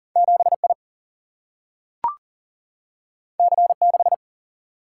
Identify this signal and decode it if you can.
Morse code, heard as a keyed tone, about 40 wpm, 700 Hz. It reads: 7I C6
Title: Single-Letter Single-Number 40wpm